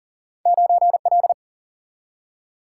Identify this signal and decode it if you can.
Morse code, heard as a keyed tone, about 41 wpm, 700 Hz. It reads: 9L
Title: Single-Letter Single-Number 40wpm